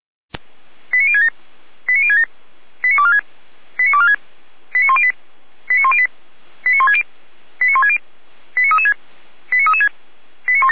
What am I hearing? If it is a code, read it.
Five-tone selective call (ZVEI), heard as quick runs of five tones, tones of about 70 ms. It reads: 89067 89067 89246 89246 89108 89108 89160 89160 89336 89336
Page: /455/SDR/ZVEI/89099/2025.08.13